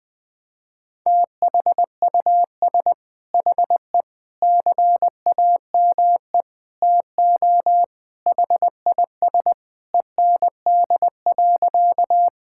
Morse code, heard as a keyed tone, about 21 wpm, 700 Hz. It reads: THUS HE CAME TO HIS END.
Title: Sentences from Top 1000 Words - Review Part 5 - 20wpm